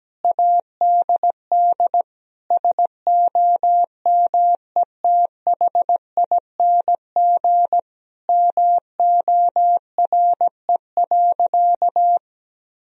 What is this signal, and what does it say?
Morse code, heard as a keyed tone, about 17 wpm, 700 Hz. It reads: ADD SOMETHING MORE.
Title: Sentences from Top 1000 Words - Review Part 1 - 17wpm